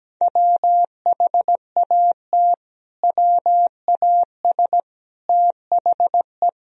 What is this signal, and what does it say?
Morse code, heard as a keyed tone, about 17 wpm, 700 Hz. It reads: WHAT WAS THE
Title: Sentences from Top 1000 Words - Review Part 3 - 17wpm